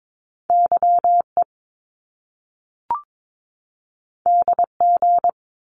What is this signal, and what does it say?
Morse code, heard as a keyed tone, about 22 wpm, 700 Hz. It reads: YE DG
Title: Top Two Letter Combinations in Words 22wpm